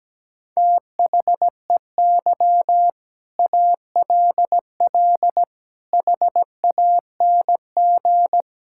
Morse code, heard as a keyed tone, about 17 wpm, 700 Hz. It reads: THEY ALL HANG
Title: Sentences from Top 1600 Words 17wpm